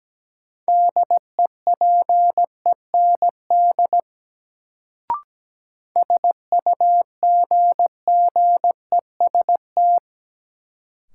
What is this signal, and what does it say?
Morse code, heard as a keyed tone, about 17 wpm, 700 Hz. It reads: DEPEND SUGGEST
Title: Sentences from Top 1300 Words - New Word Drill - 17wpm